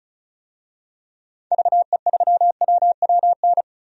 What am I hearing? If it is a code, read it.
Morse code, heard as a keyed tone, about 36 wpm, 690 Hz. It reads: VE3WWN
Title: DXCC-100 Call Signs - 6 Characters - 35wpm